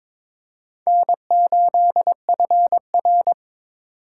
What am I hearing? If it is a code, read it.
Morse code, heard as a keyed tone, about 22 wpm, 700 Hz. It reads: N8FR
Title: USA Call Signs - 4 Characters - 4x Character Spacing - 22wpm